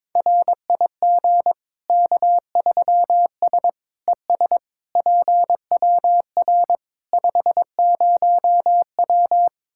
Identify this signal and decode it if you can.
Morse code, heard as a keyed tone, about 22 wpm, 700 Hz. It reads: RIG K3S ES PWR 50W
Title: Practice QSO - Exchange 2 - Info - 22wpm